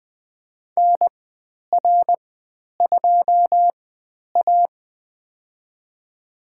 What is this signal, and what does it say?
Morse code, heard as a keyed tone, about 20 wpm, spaced as for 10 wpm, 700 Hz. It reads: NR2A
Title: 109 Lesson 18g - Letter m - Call Signs 20wpm